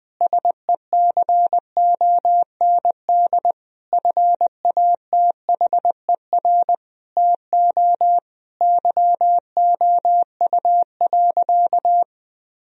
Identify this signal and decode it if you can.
Morse code, heard as a keyed tone, about 20 wpm, 700 Hz. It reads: SECOND FATHER TO YOU.
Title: Sentences from Top 1000 Words - Review Part 1 - 20wpm